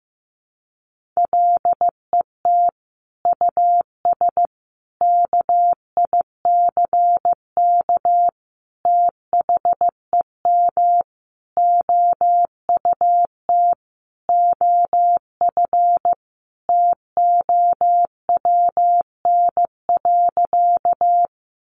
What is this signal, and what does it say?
Morse code, heard as a keyed tone, about 15 wpm, 700 Hz. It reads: LET US KICK THEM OUT OF TOWN.
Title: Sentences from Top 1300 Words 15wpm